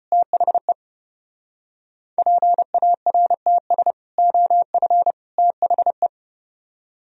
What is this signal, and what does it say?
Morse code, heard as a keyed tone, about 34 wpm, 700 Hz. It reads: THE PARTS OF THE